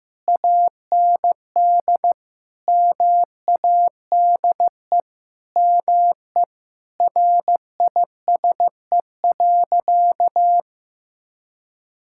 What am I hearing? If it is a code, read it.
Morse code, heard as a keyed tone, about 15 wpm, 700 Hz. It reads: AND MADE ME RISE.
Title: Sentences from Top 1000 Words - Review Part 6 - 15wpm